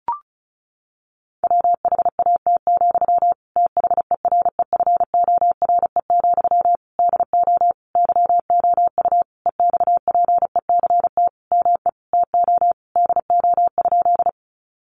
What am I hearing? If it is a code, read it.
Morse code, heard as a keyed tone, about 34 wpm, 700 Hz. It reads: WHAT, THEREFORE, DO YOU EXPECT ME TO DO?
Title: Sentences from Top 1100 Words 35wpm